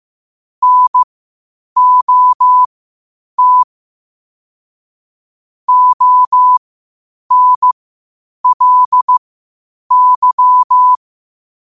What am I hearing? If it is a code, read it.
Morse code, heard as a keyed tone, about 15 wpm, spaced as for 8 wpm, 1000 Hz. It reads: NOT ONLY
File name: qotdfiles / 8wpm